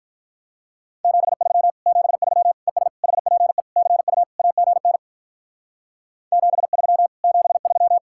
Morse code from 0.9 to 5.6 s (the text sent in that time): 73 73 ES HPE CU AGN